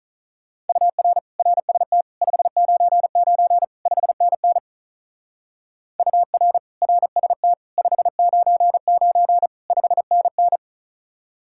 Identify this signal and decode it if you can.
Morse code, heard as a keyed tone, about 44 wpm, 700 Hz. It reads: UR RST 599 5NN UR RST 599 5NN